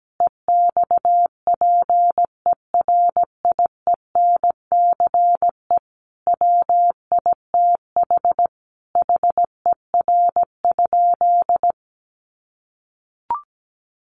Morse code, heard as a keyed tone, about 17 wpm, 700 Hz. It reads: EXPERIENCE WITH HER?
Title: Sentences from Top 600 Words 17wpm